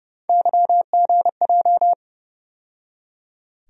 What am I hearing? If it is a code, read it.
Morse code, heard as a keyed tone, about 30 wpm, 700 Hz. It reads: YGJ
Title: USA Call Sign Postfixes - 3 Letters 30wpm